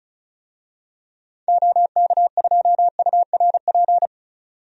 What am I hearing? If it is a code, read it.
Morse code, heard as a keyed tone, about 36 wpm, 700 Hz. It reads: OK2URP
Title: DXCC-100 Call Signs - 6 Characters - 35wpm